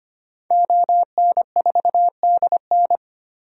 Morse code, heard as a keyed tone, about 25 wpm, 700 Hz. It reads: ON4DN
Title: DXCC-100 Call Signs - 5 Characters - 25wpm